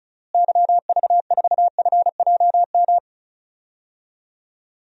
Morse code, heard as a keyed tone, about 34 wpm, 700 Hz. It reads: YV4FJM
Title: DXCC-100 Call Signs - 6 Characters - Repeated 2x - 35wpm